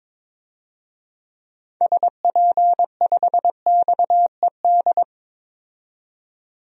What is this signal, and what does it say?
Morse code, heard as a keyed tone, about 22 wpm, 700 Hz. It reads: SP5XED
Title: DXCC-100 Call Signs - 6 Characters - 22wpm